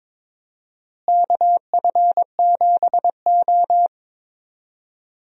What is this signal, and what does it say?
Morse code, heard as a keyed tone, about 22 wpm, 700 Hz. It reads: KF7O
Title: USA Call Signs - 4 Characters - Repeated 3x - 22wpm